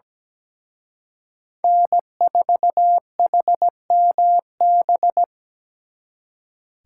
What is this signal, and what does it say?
Morse code, heard as a keyed tone, about 20 wpm, 700 Hz. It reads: N4HMB
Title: Speed Racing - USA 1x3 Call Signs 17wpm